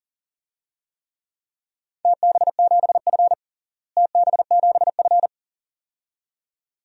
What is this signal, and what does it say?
Morse code, heard as a keyed tone, about 40 wpm, 700 Hz. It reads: TB7F TB7F
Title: DXCC-100 Call Signs - 4 Characters - Repeated 2x - 40wpm